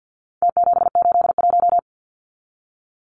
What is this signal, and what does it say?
Morse code, heard as a keyed tone, about 50 wpm, 700 Hz. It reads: T781
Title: Numbers - Contest Style - 4-Digit 50wpm